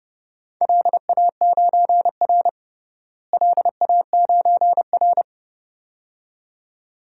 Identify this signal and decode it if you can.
Morse code, heard as a keyed tone, about 32 wpm, 700 Hz. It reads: LA9R LA9R
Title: DXCC-100 Call Signs - 4 Characters - Repeated 2x - 30wpm